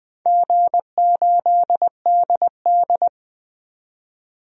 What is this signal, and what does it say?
Morse code, heard as a keyed tone, about 20 wpm, 700 Hz. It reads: G8DD
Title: DXCC-100 Call Signs - 4 Characters - 20wpm